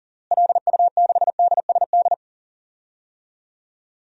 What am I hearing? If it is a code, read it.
Morse code, heard as a keyed tone, about 37 wpm, 700 Hz. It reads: LU6DSD
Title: DXCC-100 Call Signs - 6 Characters - Repeated 3x - 40wpm